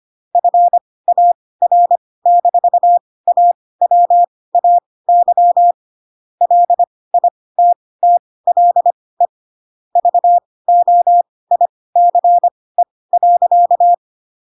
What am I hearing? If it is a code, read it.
Morse code, heard as a keyed tone, about 25 wpm, spaced as for 18 wpm, 700 Hz. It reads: FAR-AWAY LITTLE VOICE.
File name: Pinocchio / Pinocchio ch1 12 25 ik1hge